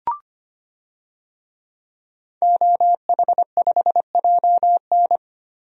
Morse code, heard as a keyed tone, about 25 wpm, 700 Hz. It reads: OH5JN
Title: DXCC-100 Call Signs - 5 Characters - 25wpm